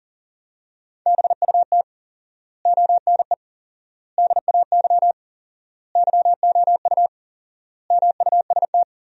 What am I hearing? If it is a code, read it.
Morse code, heard as a keyed tone, about 40 wpm, 700 Hz. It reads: BUT ONE DAY YOU MUST
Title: 4x Word-Spacing - Sentences from Top 400 Words - 40wpm